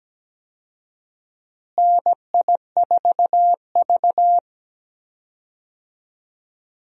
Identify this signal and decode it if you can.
Morse code, heard as a keyed tone, about 17 wpm, 700 Hz. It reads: NI4V
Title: USA Call Signs - 4 Characters - Repeated 2x - 17wpm